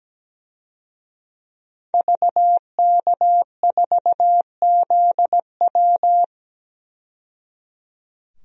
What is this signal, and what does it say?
Morse code, heard as a keyed tone, about 17 wpm, 700 Hz. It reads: VK4ZW
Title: DXCC-100 Call Signs - 5 Characters - 17wpm